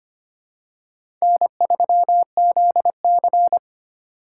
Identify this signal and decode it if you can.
Morse code, heard as a keyed tone, about 25 wpm, 700 Hz. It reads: N3ZC
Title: USA Call Signs - 4 Characters - 4x Character Spacing - 25wpm